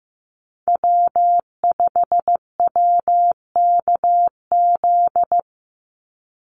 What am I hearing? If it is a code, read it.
Morse code, heard as a keyed tone, about 15 wpm, 700 Hz. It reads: W5WKZ
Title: USA Call Signs 15wpm